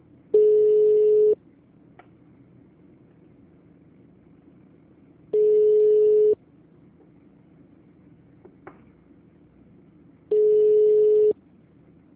Ringback tone (Europe, 425 Hz).